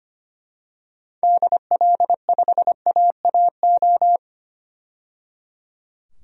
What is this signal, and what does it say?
Morse code, heard as a keyed tone, about 25 wpm, 700 Hz. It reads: DL5AAO
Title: DXCC-100 Call Signs - 6 Characters - 25wpm